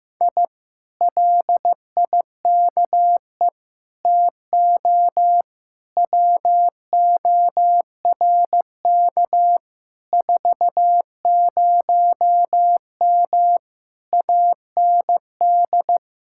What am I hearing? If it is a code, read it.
Morse code, heard as a keyed tone, about 15 wpm, 700 Hz. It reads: I LIKE TO WORK 40M AND
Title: Speed-Racing Common QSO Phrases Encore - 15wpm